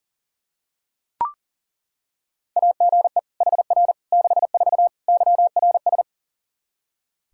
Morse code, heard as a keyed tone, about 40 wpm, 700 Hz. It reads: AGE HR 64 YRS
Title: K1USN SST QSO Practice 40wpm